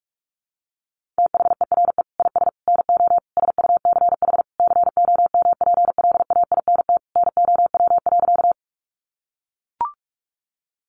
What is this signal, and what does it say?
Morse code, heard as a keyed tone, about 44 wpm, 700 Hz. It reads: THERE IS NO SUCH COMPLAINT NOW.
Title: Sentences from Top 1400 Words 45wpm